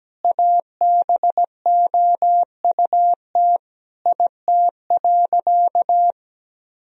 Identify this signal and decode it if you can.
Morse code, heard as a keyed tone, about 17 wpm, 700 Hz. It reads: ABOUT IT.
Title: Sentences from Top 1000 Words - Review Part 2 - 17wpm